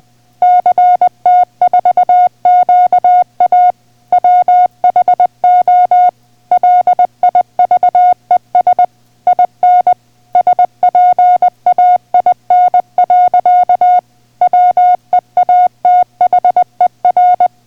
Morse code, heard as a keyed tone, about 20 wpm, 720 Hz. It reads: CT4QA WHO LIVES IN SPAIN. WEATHER